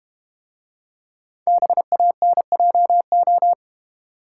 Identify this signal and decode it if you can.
Morse code, heard as a keyed tone, about 32 wpm, 700 Hz. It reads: BANJO